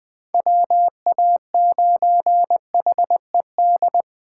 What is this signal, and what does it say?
Morse code, heard as a keyed tone, about 20 wpm, 700 Hz. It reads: WA9HED
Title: 2023 SOTA Chaser Call Signs - 20wpm